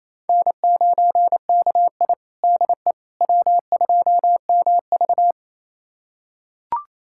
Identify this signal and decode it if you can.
Morse code, heard as a keyed tone, about 28 wpm, 700 Hz. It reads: N9KI DE W2MV
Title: USA Callsign DE Callsign - 4 Characters - 28wpm